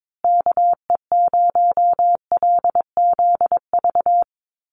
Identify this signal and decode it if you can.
Morse code, heard as a keyed tone, about 22 wpm, 700 Hz. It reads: KE0LZV
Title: USA Call Signs Repeated 2x 22wpm